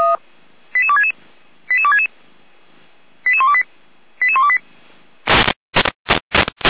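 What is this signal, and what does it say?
Five-tone selective call (ZVEI), heard as quick runs of five tones, tones of about 70 ms. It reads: 88266 88266 88128 88128
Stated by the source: ZVEI five-tone selective call received on SDR.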